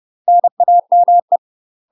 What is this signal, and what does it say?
Morse code, heard as a keyed tone, about 30 wpm, 700 Hz. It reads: NAME
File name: UK MASTER QSOs TRAINING / 30WPM / UK MASTER QSOs 0039